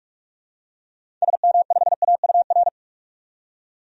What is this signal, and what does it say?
Morse code, heard as a keyed tone, about 46 wpm, 700 Hz. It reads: SM5AUR
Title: DXCC-100 Call Signs - 6 Characters - 45wpm